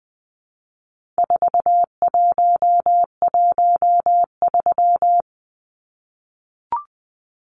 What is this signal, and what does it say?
Morse code, heard as a keyed tone, about 20 wpm, 700 Hz. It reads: 4113
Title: Numbers - 4-Digit 20wpm